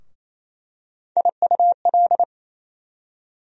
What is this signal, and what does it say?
Morse code, heard as a keyed tone, about 28 wpm, 700 Hz. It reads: IUL
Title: USA Call Sign Postfixes - 3 Letters 28wpm